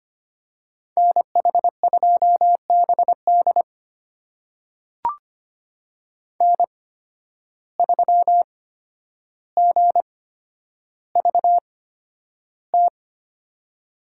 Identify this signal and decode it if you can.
Morse code, heard as a keyed tone, about 25 wpm, spaced as for 11 wpm, 700 Hz. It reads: NH2BD N3GVT
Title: USA Call Signs - 5 Characters - 8x Character Spacing - 25wpm